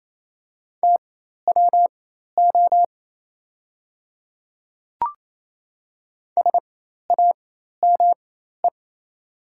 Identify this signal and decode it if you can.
Morse code, heard as a keyed tone, about 27 wpm, spaced as for 13 wpm, 690 Hz. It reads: TWO SAME
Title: 4x Character Spacing - Top 300 Words - 28wpm